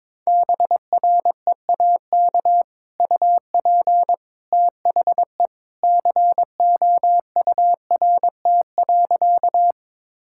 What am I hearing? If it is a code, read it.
Morse code, heard as a keyed tone, about 22 wpm, 700 Hz. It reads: BREAK UP THE COURT.
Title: Sentences from Top 1600 Words 22wpm